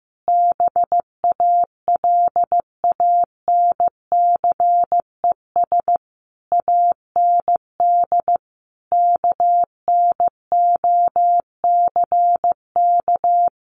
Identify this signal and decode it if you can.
Morse code, heard as a keyed tone, about 15 wpm, 700 Hz. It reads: BALANCES AND KNOCK
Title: A Sound Of Thunder 15wpm